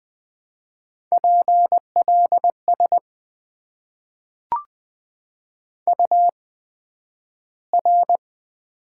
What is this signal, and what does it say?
Morse code, heard as a keyed tone, about 20 wpm, spaced as for 11 wpm, 700 Hz. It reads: PLS UR
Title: Top 90 QSO Elements - 8x Character Spacing - 20wpm